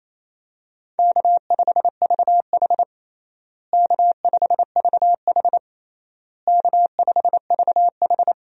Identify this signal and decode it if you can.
Morse code, heard as a keyed tone, about 27 wpm, 700 Hz. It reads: K5VH K5VH K5VH
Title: USA Call Signs - 4 Characters - Repeated 3x - 28wpm